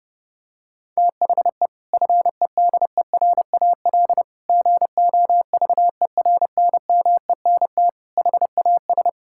Morse code, heard as a keyed tone, about 32 wpm, 700 Hz. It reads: THE FEDERAL GOVERNMENT HAS
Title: Sentences from Top 1000 Words - Review Part 5 - 30wpm